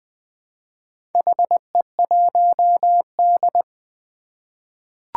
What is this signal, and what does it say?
Morse code, heard as a keyed tone, about 20 wpm, 700 Hz. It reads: HE1D
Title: Four Challenging Letter-Number Combinations 20wpm